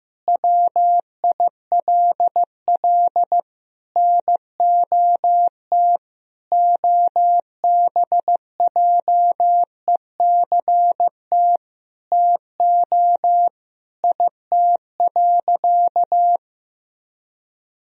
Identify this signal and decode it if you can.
Morse code, heard as a keyed tone, about 16 wpm, 700 Hz. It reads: WILL NOT OBJECT TO IT.
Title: Sentences from Top 1000 Words - Review Part 3 - 15wpm